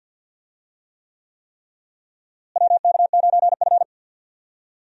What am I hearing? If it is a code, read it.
Morse code, heard as a keyed tone, about 46 wpm, 700 Hz. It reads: WK9F
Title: USA Call Signs - 4 Characters - Repeated 3x - 50wpm